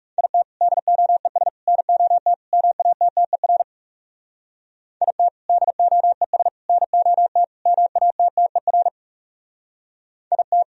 Morse code, heard as a keyed tone, about 37 wpm, 700 Hz. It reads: IT DOES NOT MATTER IT DOES NOT MATTER IT
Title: Speed-Racing - Sets of 4 Words 30wpm